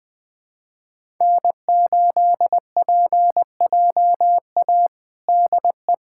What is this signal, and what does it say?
Morse code, heard as a keyed tone, about 20 wpm, 700 Hz. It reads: N8PJA DE
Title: USA Callsign DE Callsign - 5 Characters - 20wpm